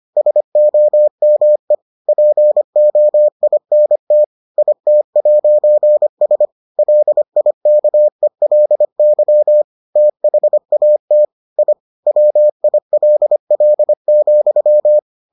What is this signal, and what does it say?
Morse code, heard as a keyed tone, about 25 wpm, 600 Hz. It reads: SOME POINT IT'S LIKELY THAT I WILL,